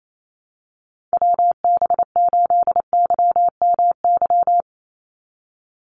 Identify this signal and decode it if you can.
Morse code, heard as a keyed tone, about 28 wpm, 700 Hz. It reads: WB8YMY
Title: USA Call Signs 28wpm